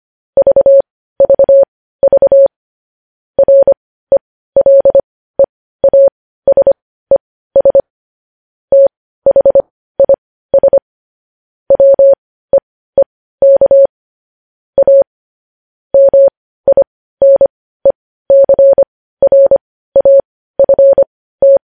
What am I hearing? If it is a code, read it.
Morse code, heard as a keyed tone, about 25 wpm, spaced as for 15 wpm, 550 Hz. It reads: VVV RELEASES THIS WEEK A MINECRAFT
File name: podcast / News headlines-1x-25x15